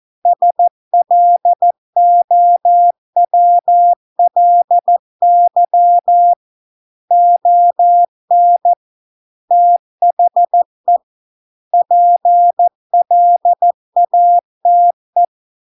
Morse code, heard as a keyed tone, about 14 wpm, 700 Hz. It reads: SLOWLY ON THE PLATE